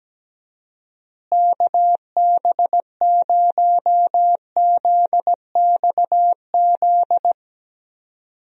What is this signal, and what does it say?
Morse code, heard as a keyed tone, about 17 wpm, 700 Hz. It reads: KB0ZXZ
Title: USA Call Signs - 6 Characters - Repeated 3x - 17wpm